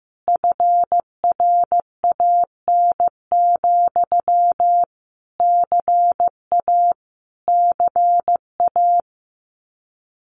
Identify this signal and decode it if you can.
Morse code, heard as a keyed tone, about 14 wpm, 700 Hz. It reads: FRAN, CA CA
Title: Common QSO Phrases Encore 15wpm